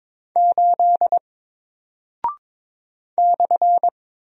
Morse code, heard as a keyed tone, about 22 wpm, 700 Hz. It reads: 8 /
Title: Single Character (Letter:Number:Punctuation:Prosign) 22wpm